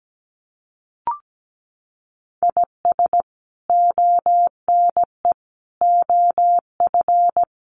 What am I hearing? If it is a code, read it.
Morse code, heard as a keyed tone, about 17 wpm, 700 Hz. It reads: IS ONE OF
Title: Sets of 5 Words Encore2 17wpm